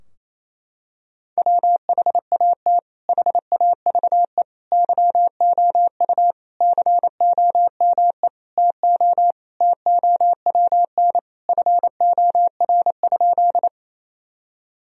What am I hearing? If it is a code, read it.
Morse code, heard as a keyed tone, about 28 wpm, 700 Hz. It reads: WHAT HAVE YOU COME TO TOWN FOR?
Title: Sentences from Top 1000 Words - Review Part 2 - 28wpm